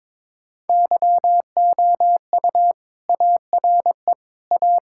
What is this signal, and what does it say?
Morse code, heard as a keyed tone, about 22 wpm, 700 Hz. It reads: YOU ARE A
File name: practice / 22wpm / Sentences with 3-Word Phrases - 22wpm